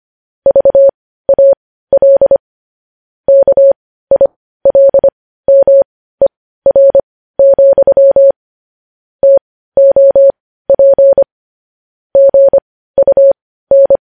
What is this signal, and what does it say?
Morse code, heard as a keyed tone, about 25 wpm, spaced as for 15 wpm, 550 Hz. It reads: VAL KILMER, TOP GUN
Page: news headlines in morse code at 15 wpm